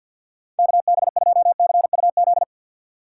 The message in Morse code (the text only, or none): KB2CUZ